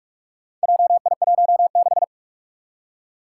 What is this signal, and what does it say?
Morse code, heard as a keyed tone, about 42 wpm, 700 Hz. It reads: JI16
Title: Four Challenging Letter-Number Combinations 45wpm